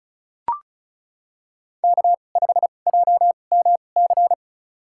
Morse code, heard as a keyed tone, about 35 wpm, spaced as for 25 wpm, 700 Hz. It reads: K5JMC
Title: USA Call Signs - 5 Characters - 2x Character Spacing - 35wpm